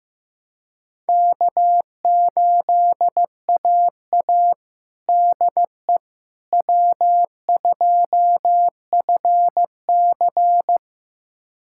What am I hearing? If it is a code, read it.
Morse code, heard as a keyed tone, about 15 wpm, 700 Hz. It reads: K8AA DE W2FC
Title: USA Callsign DE Callsign - 4 Characters - 15wpm